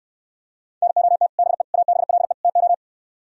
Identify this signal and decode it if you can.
Morse code, heard as a keyed tone, about 67 wpm, 700 Hz. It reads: NOT BE ABLE TO